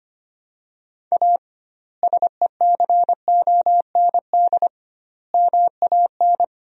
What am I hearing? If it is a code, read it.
Morse code, heard as a keyed tone, about 25 wpm, 700 Hz. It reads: A SECOND MAN
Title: 2x Word-Spacing - Sentences from Top 300 Words - 25wpm